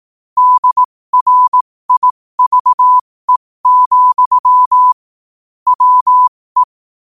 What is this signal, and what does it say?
Morse code, heard as a keyed tone, about 18 wpm, spaced as for 15 wpm, 1000 Hz. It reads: DRIVE, WE
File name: qotdfiles / 15wpm